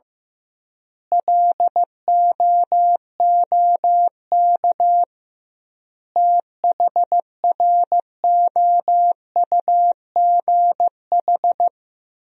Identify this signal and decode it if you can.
Morse code, heard as a keyed tone, about 15 wpm, 700 Hz. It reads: LOOK THROUGH
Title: 2x Word-Spacing - Sentences from Top 200 Words - 15wpm